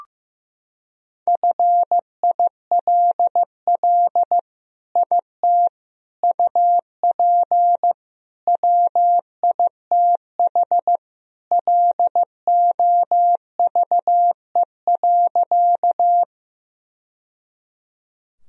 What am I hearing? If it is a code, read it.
Morse code, heard as a keyed tone, about 16 wpm, 700 Hz. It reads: FILL IT UP WITH LOVE.
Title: Sentences from Top 1000 Words - Review Part 3 - 15wpm